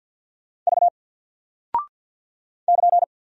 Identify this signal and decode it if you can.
Morse code, heard as a keyed tone, about 50 wpm, 700 Hz. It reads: V <KN>
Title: Single Character (Letter:Number:Punctuation:Prosign) 50wpm